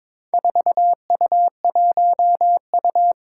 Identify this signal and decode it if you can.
Morse code, heard as a keyed tone, about 22 wpm, 700 Hz. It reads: 4U1U
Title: Four Challenging Letter-Number Combinations 22wpm